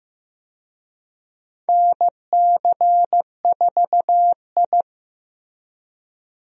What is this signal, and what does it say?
Morse code, heard as a keyed tone, about 16 wpm, 700 Hz. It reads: NC4I
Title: USA Call Signs - 4 Characters - Repeated 2x - 15wpm